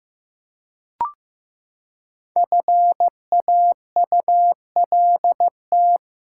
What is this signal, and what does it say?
Morse code, heard as a keyed tone, about 15 wpm, 700 Hz. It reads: FAULT
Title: Five Letter Words - 15wpm